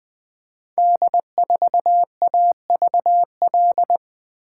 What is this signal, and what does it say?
Morse code, heard as a keyed tone, about 20 wpm, 700 Hz. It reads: D4AVL
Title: Code Groups - 5 Letter:Numbers - 20wpm